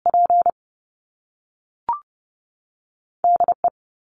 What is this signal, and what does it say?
Morse code, heard as a keyed tone, about 30 wpm, 700 Hz. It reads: P DE
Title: Top Two Letter Combinations in Words 30wpm